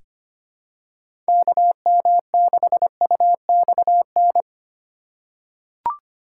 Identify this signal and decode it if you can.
Morse code, heard as a keyed tone, about 25 wpm, 700 Hz. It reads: KM6UXN
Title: USA Call Signs - 6 Characters - 4x Character Spacing - 25wpm